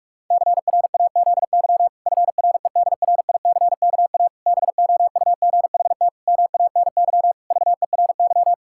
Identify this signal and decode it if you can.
Morse code, heard as a keyed tone, about 46 wpm, 700 Hz. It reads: CRAZY FREDRICKA BOUGHT MANY VERY
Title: Pangram Challenge v2 - 45wpm